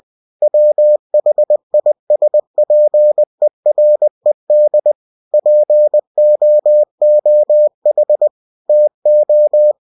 Morse code, heard as a keyed tone, about 20 wpm, 600 Hz. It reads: WHISPERED POOH TO
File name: cw-books / winnie-the-pooh / winnie-20wpm-page-0082